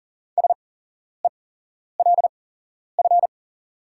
Morse code, heard as a keyed tone, about 40 wpm, spaced as for 11 wpm, 690 Hz. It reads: SELF
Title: 8x Character Spacing - Top 500 Words - 40wpm